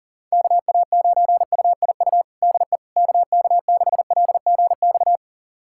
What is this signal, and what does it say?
Morse code, heard as a keyed tone, about 40 wpm, 700 Hz. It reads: KA9UIU DE KK6LGX
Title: USA Callsign DE Callsign - 6 Characters - 40wpm